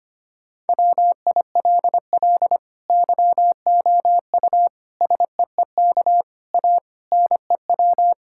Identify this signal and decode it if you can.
Morse code, heard as a keyed tone, about 25 wpm, 700 Hz. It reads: WILL YOU SEEK A NEW
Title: Sentences from Top 1600 Words 25wpm